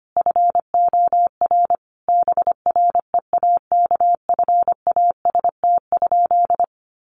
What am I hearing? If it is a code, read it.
Morse code, heard as a keyed tone, about 25 wpm, 700 Hz. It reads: FOR BREAKFAST?
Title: Sentences from Top 1500 Words 25wpm